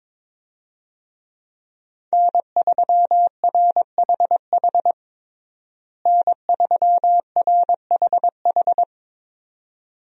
Morse code, heard as a keyed tone, about 22 wpm, 700 Hz. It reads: N3RHH N3RHH
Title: USA Call Signs - 5 Characters - Repeated 2x - 22wpm